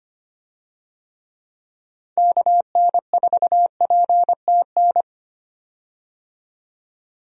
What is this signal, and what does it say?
Morse code, heard as a keyed tone, about 25 wpm, 700 Hz. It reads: KN4PTN
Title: USA Call Signs - 6 Characters - 25wpm